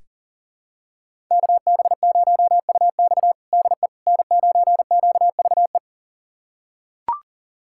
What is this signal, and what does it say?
Morse code, heard as a keyed tone, about 40 wpm, 700 Hz. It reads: KB0UX DE N9QVE
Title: USA Callsign DE Callsign 40wpm